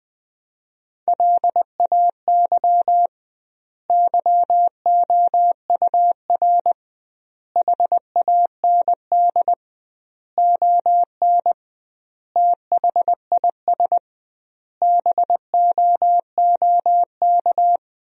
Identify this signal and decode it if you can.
Morse code, heard as a keyed tone, about 20 wpm, 700 Hz. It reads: LAY YOUR HAND ON THIS BOOK
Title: 2x Word-Spacing - Sentences from Top 400 Words - 20wpm